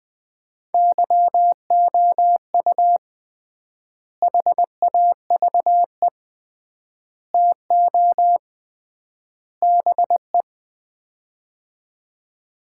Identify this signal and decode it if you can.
Morse code, heard as a keyed tone, about 20 wpm, 700 Hz. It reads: YOU HAVE TO BE
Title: 3x Word-Spacing - Sets of 4 Words - 20wpm